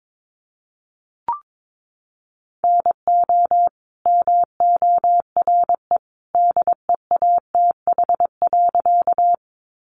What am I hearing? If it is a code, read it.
Morse code, heard as a keyed tone, about 22 wpm, 700 Hz. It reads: NO MORE DEATH.
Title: Sentences from Top 1100 Words 22wpm